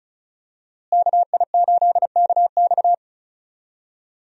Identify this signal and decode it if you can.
Morse code, heard as a keyed tone, about 34 wpm, 700 Hz. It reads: KI8KX
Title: USA Call Signs - 5 Characters - Repeated 3x - 35wpm